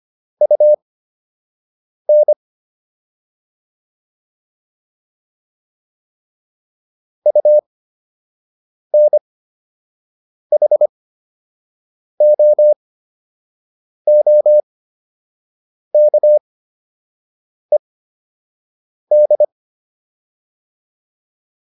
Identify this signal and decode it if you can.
Morse code, heard as a keyed tone, about 25 wpm, spaced as for 6 wpm, 600 Hz. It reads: UN UNHOOKED